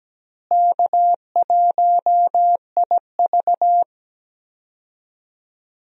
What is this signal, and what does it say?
Morse code, heard as a keyed tone, about 17 wpm, 700 Hz. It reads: K1IV
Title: USA Call Signs - 4 Characters - Repeated 3x - 17wpm